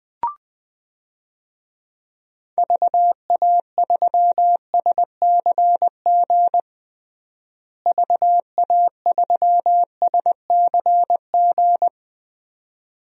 Morse code, heard as a keyed tone, about 21 wpm, 700 Hz. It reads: VA3SCG VA3SCG
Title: DXCC-100 Call Signs - 6 Characters - Repeated 2x - 20wpm